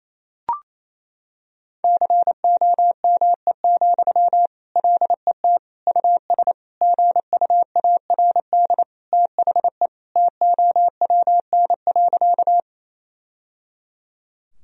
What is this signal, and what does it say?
Morse code, heard as a keyed tone, about 27 wpm, 700 Hz. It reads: COME, LET US GUARD THE TOWN.
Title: Sentences from Top 1000 Words - Review Part 4 - 28wpm